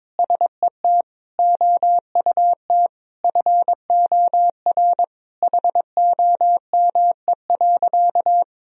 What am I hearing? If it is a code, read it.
Morse code, heard as a keyed tone, about 22 wpm, 700 Hz. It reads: SET OUT FOR HOME.
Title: Sentences from Top 1000 Words - Review Part 1 - 22wpm